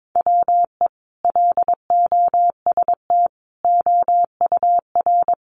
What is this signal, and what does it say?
Morse code, heard as a keyed tone, about 22 wpm, 700 Hz. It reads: WE LOST OUR
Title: Sentences from Top 1500 Words 22wpm